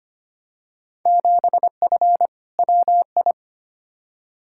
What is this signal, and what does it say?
Morse code, heard as a keyed tone, about 25 wpm, 700 Hz. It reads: 7F WI
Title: Field Day Exchanges Encore - 25wpm